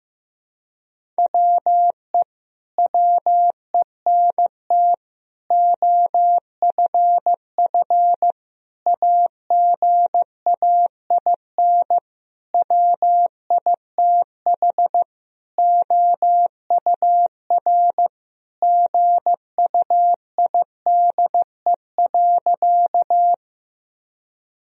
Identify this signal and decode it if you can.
Morse code, heard as a keyed tone, about 15 wpm, 700 Hz. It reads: WE WENT OFF AGAIN WITH OUR GUIDE.
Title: Sentences from Top 1000 Words - Review Part 3 - 15wpm